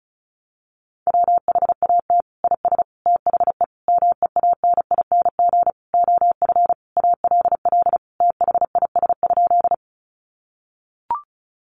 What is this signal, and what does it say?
Morse code, heard as a keyed tone, about 34 wpm, 700 Hz. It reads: WHAT IS THE MEANING OF ALL THIS?
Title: Sentences from Top 900 Words 35wpm